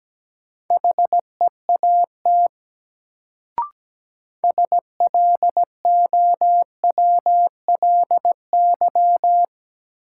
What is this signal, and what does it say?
Morse code, heard as a keyed tone, about 17 wpm, 700 Hz. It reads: HEAT SLOWLY
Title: Sentences from Top 1100 Words - New Word Drill Alt - 17wpm